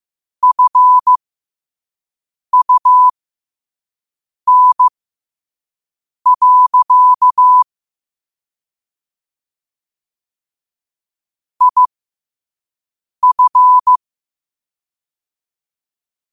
Morse code, heard as a keyed tone, about 15 wpm, spaced as for 5 wpm, 1000 Hz. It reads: FUN. IF
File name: qotdfiles / 5wpm